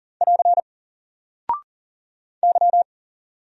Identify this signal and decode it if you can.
Morse code, heard as a keyed tone, about 40 wpm, 700 Hz. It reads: <AR> Y
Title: Single Character (Letter:Number:Punctuation:Prosign) Alt 40wpm